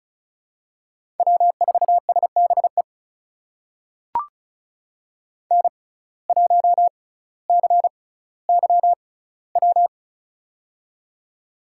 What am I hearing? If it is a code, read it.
Morse code, heard as a keyed tone, about 36 wpm, spaced as for 18 wpm, 700 Hz. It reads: W4SBE N1CYW
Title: USA Call Signs - 5 Characters - 6x Character Spacing - 35wpm